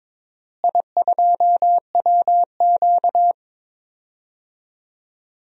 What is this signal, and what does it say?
Morse code, heard as a keyed tone, about 22 wpm, 700 Hz. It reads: I2WQ
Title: DXCC-100 Call Signs - 4 Characters - Repeated 3x - 22wpm